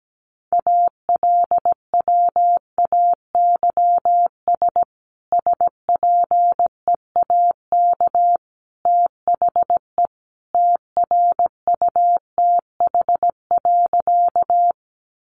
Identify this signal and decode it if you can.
Morse code, heard as a keyed tone, about 17 wpm, 700 Hz. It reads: ALWAYS SPEAK THE TRUTH.
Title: Sentences from Top 1100 Words 17wpm